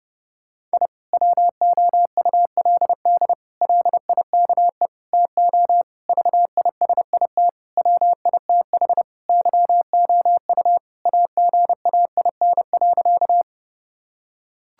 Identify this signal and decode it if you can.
Morse code, heard as a keyed tone, about 30 wpm, 700 Hz. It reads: I WOULD LIKE TO VISIT WITH YOU AGAIN.
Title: Sentences from Top 1000 Words - Review Part 6 - 30wpm